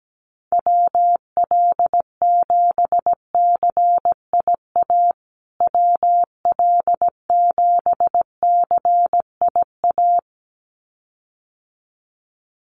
Morse code, heard as a keyed tone, about 17 wpm, 700 Hz. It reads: WL7CIA WL7CIA
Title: USA Call Signs Repeated 2x 17wpm